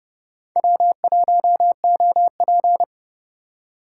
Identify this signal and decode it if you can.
Morse code, heard as a keyed tone, about 36 wpm, 700 Hz. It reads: W1OP
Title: Speed Racing - USA 1x2 Call Signs 30wpm